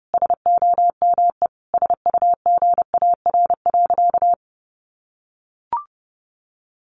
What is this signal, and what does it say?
Morse code, heard as a keyed tone, about 30 wpm, 700 Hz. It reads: SOME SUGAR.
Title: Sentences from Top 1200 Words 30wpm